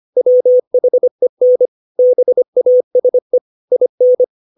Morse code, heard as a keyed tone, about 25 wpm, 500 Hz. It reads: WHEN BASE IN
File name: 632 words / 632 Words 25WPM